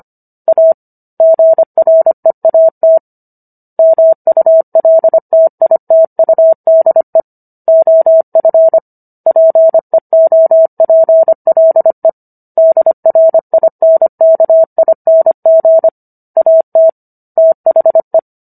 Morse code, heard as a keyed tone, about 25 wpm, 650 Hz. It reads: A GREAT MULTITUDE OF PEOPLE DRINKING AT THE